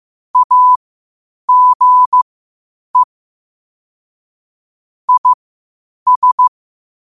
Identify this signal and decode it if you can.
Morse code, heard as a keyed tone, about 15 wpm, spaced as for 8 wpm, 1000 Hz. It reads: AGE IS
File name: qotdfiles / 8wpm